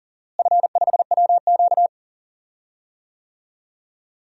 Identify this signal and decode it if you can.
Morse code, heard as a keyed tone, about 40 wpm, 700 Hz. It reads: F5WQ
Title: DXCC-100 Call Signs - 4 Characters - Repeated 2x - 40wpm